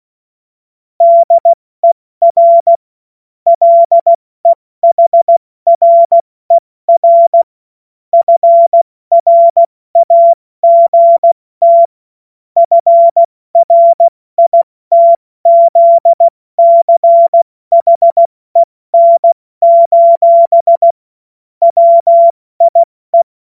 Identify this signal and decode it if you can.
Morse code, heard as a keyed tone, about 16 wpm, 680 Hz. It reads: DER LEHRER FRAGT FRITZCHEN: WIE
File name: Morsekurs / Witze / Witz 14. September 2021, 70 F80